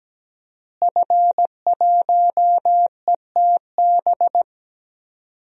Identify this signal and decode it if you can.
Morse code, heard as a keyed tone, about 17 wpm, 700 Hz. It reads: F1ETB
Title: DXCC-100 Call Signs - 5 Characters - Repeated 3x - 17wpm